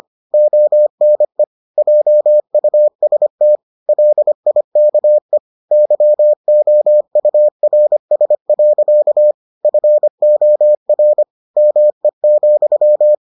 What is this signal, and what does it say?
Morse code, heard as a keyed tone, about 25 wpm, 600 Hz. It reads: ONE JUST LIKE YOURS. FOR ME,